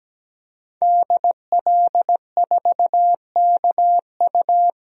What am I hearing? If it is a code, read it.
Morse code, heard as a keyed tone, about 17 wpm, 700 Hz. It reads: DL4KU
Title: DXCC-100 Call Signs - 5 Characters - 17wpm